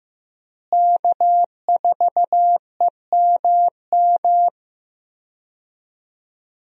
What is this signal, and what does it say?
Morse code, heard as a keyed tone, about 15 wpm, 700 Hz. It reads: K4EMM
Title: USA Call Signs - 5 Characters - Repeated 2x - 15wpm